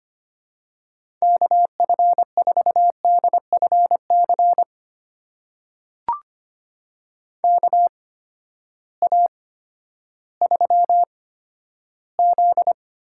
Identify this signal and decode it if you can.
Morse code, heard as a keyed tone, about 25 wpm, spaced as for 12 wpm, 700 Hz. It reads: KF4DFC KA3Z
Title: USA Call Signs - 6 Characters - 8x Character Spacing - 25wpm